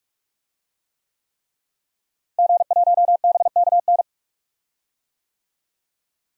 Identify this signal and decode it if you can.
Morse code, heard as a keyed tone, about 42 wpm, 700 Hz. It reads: G1BKN
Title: DXCC-100 Call Signs - 5 Characters - Repeated 3x - 45wpm